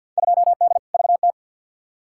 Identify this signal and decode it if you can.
Morse code, heard as a keyed tone, about 51 wpm, 700 Hz. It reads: 2D VT
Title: Field Day Exchanges Encore - 50wpm